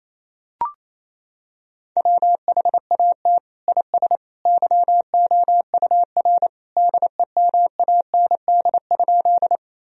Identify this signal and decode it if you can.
Morse code, heard as a keyed tone, about 28 wpm, 700 Hz. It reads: WHAT IS YOUR DEMAND?
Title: Sentences from Top 1000 Words - Review Part 4 - 28wpm